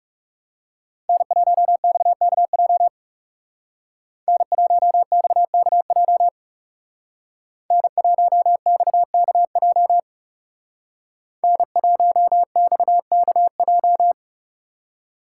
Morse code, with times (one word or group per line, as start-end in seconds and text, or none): 1.1-2.9 N1XKJ
4.3-6.3 N1XKJ
7.7-10.0 N1XKJ
11.4-14.1 N1XKJ